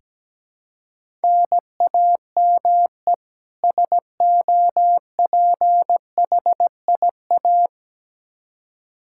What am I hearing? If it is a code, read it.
Morse code, heard as a keyed tone, about 17 wpm, 700 Hz. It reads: NAME SOPHIA
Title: Practice QSO - Exchange 1 - Beginning - 17wpm